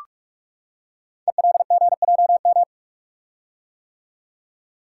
Morse code, heard as a keyed tone, about 44 wpm, 700 Hz. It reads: ELGJM
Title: Code Groups - 5 Letters - 45wpm